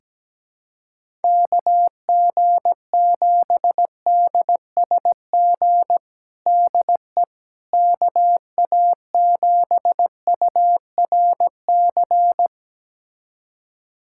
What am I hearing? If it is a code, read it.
Morse code, heard as a keyed tone, about 17 wpm, 700 Hz. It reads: KG7DSG DE KA7URC
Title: USA Callsign DE Callsign - 6 Characters - 17wpm